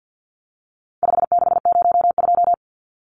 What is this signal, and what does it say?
Morse code, heard as a keyed tone, about 50 wpm, 700 Hz. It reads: 5602
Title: Numbers - 4-Digit 50wpm